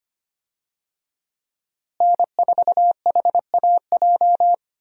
Morse code, heard as a keyed tone, about 25 wpm, 700 Hz. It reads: N4HAJ
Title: USA Call Signs - 5 Characters - Repeated 3x - 25wpm